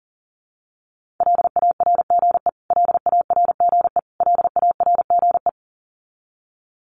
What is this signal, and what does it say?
Morse code, heard as a keyed tone, about 37 wpm, 700 Hz. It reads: LARGE LARGE LARGE
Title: Top 200 Words Repeated 40wpm